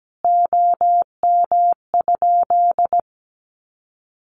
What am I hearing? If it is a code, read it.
Morse code, heard as a keyed tone, about 17 wpm, 700 Hz. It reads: OM?
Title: Common QSO Phrases 17wpm